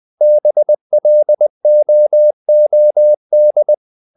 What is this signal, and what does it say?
Morse code, heard as a keyed tone, about 20 wpm, 600 Hz. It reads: BLOOD
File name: the old man and the sea part 2 20wpm